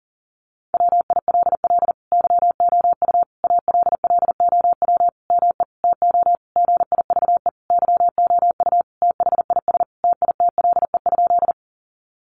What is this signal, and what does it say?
Morse code, heard as a keyed tone, about 40 wpm, 700 Hz. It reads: WILL YOU ALLOW ME TO GIVE YOU THIS TITLE?
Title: Sentences from Top 1200 Words 40wpm